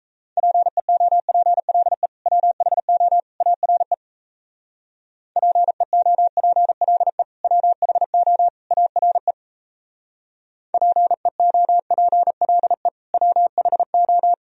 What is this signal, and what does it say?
Morse code, heard as a keyed tone, about 36 wpm, 700 Hz. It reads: PEOPLE WHO ARE PEOPLE WHO ARE PEOPLE WHO